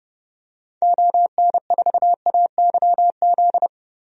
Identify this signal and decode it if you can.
Morse code, heard as a keyed tone, about 30 wpm, 700 Hz. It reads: ON4AYZ
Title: DXCC-100 Call Signs - 6 Characters - 30wpm